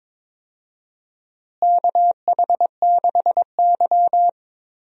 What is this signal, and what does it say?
Morse code, heard as a keyed tone, about 22 wpm, 700 Hz. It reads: KH6Y
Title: DXCC-100 Call Signs - 4 Characters - Repeated 2x - 22wpm